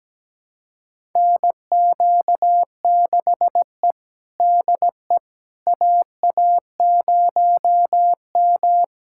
Morse code, heard as a keyed tone, about 17 wpm, 700 Hz. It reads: NQ6E DE AA0M
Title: USA Callsign DE Callsign - 4 Characters - 17wpm